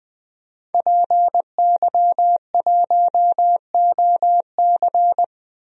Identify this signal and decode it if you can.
Morse code, heard as a keyed tone, about 21 wpm, 700 Hz. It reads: PY1OC
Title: DXCC-100 Call Signs - 5 Characters - 20wpm